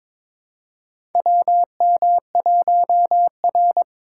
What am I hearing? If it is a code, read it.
Morse code, heard as a keyed tone, about 22 wpm, 700 Hz. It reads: WM1R
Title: USA Call Signs - 4 Characters - 4x Character Spacing - 22wpm